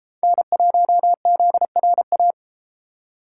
Morse code, heard as a keyed tone, about 33 wpm, 700 Hz. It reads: N1ZRA